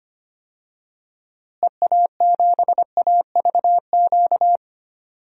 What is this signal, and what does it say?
Morse code, heard as a keyed tone, about 25 wpm, 700 Hz. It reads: EA7AVQ
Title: DXCC-100 Call Signs - 6 Characters - Repeated 3x - 25wpm